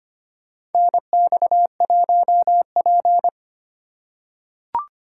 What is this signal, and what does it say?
Morse code, heard as a keyed tone, about 25 wpm, 700 Hz. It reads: NX1P
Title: USA Call Signs - 4 Characters - 2x Character Spacing - 25wpm